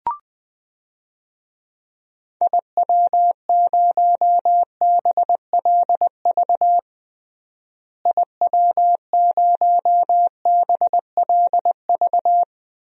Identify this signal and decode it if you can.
Morse code, heard as a keyed tone, about 21 wpm, 700 Hz. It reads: IW0BLV IW0BLV
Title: DXCC-100 Call Signs - 6 Characters - Repeated 2x - 20wpm